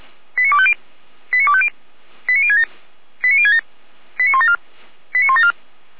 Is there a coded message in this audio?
Yes: five-tone selective call (ZVEI) from 0.4 to 5.5 s, heard as quick runs of five tones, tones of about 70 ms. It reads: 89250 89250 89067 89067 89174 89174